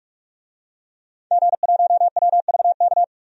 Morse code, heard as a keyed tone, about 46 wpm, 700 Hz. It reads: G1WVK
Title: DXCC-100 Call Signs - 5 Characters - 45wpm